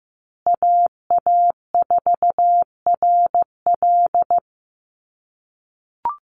Morse code, heard as a keyed tone, about 15 wpm, 700 Hz. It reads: AA4RL
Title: USA Call Signs Repeated 2x 15wpm